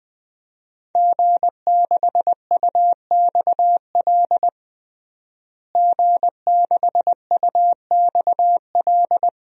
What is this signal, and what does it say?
Morse code, heard as a keyed tone, about 20 wpm, 700 Hz. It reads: G6UXL G6UXL
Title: DXCC-100 Call Signs - 5 Characters - Repeated 3x - 20wpm